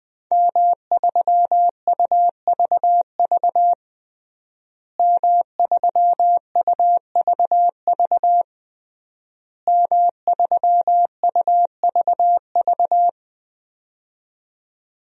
Morse code, heard as a keyed tone, about 20 wpm, 700 Hz. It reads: M3UVV M3UVV M3UVV
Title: DXCC-100 Call Signs - 5 Characters - Repeated 3x - 20wpm